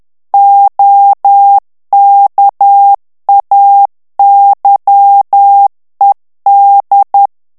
Morse code, heard as a keyed tone, about 10 wpm, 800 Hz. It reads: OKAYED
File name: ANGLAIS texte clair 3 vitesse 10 mots minute